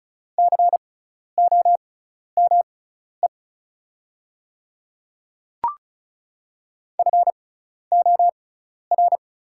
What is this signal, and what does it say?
Morse code, heard as a keyed tone, about 36 wpm, spaced as for 12 wpm, 690 Hz. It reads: COME FOR
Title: 6x Character Spacing - Top 100 Words - 35wpm